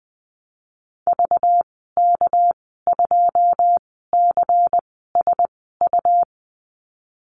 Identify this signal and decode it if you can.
Morse code, heard as a keyed tone, about 20 wpm, spaced as for 15 wpm, 700 Hz. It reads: VK2CSU
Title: 219 Lesson 35g - Number 0 - Call Signs 20wpm